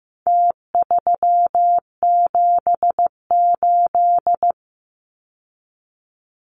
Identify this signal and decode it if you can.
Morse code, heard as a keyed tone, about 15 wpm, 700 Hz. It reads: T378
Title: Numbers - Contest Style - 4-Digit 15wpm